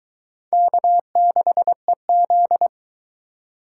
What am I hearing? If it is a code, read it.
Morse code, heard as a keyed tone, about 22 wpm, 700 Hz. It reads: K6EZ